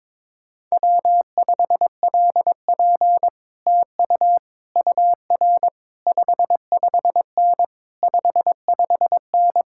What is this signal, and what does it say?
Morse code, heard as a keyed tone, about 21 wpm, 700 Hz. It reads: W5LP TU UR 55N 55N
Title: Speed-Racing Common QSO Phrases - 22wpm